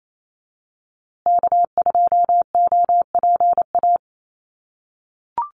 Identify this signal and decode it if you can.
Morse code, heard as a keyed tone, about 28 wpm, 700 Hz. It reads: K2OPA
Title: USA Call Signs Repeated 3x 28wpm